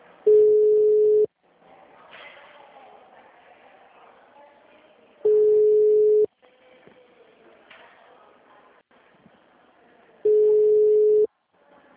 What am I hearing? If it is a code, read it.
Ringback tone (Europe, 425 Hz).
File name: out- -2001- - - .3765